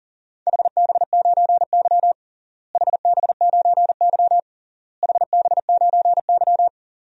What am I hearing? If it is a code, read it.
Morse code, heard as a keyed tone, about 40 wpm, 700 Hz. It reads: HB9Y HB9Y HB9Y
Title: DXCC-100 Call Signs - 4 Characters - Repeated 3x - 40wpm